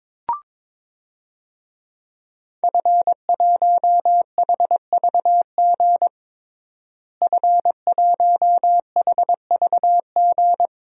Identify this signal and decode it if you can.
Morse code, heard as a keyed tone, about 22 wpm, 700 Hz. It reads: F1HVG F1HVG
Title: DXCC-100 Call Signs - 5 Characters - Repeated 2x - 22wpm